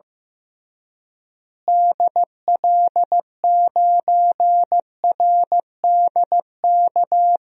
Morse code, heard as a keyed tone, about 15 wpm, 700 Hz. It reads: DL9RDK
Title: DXCC-100 Call Signs - 6 Characters - Repeated 3x - 15wpm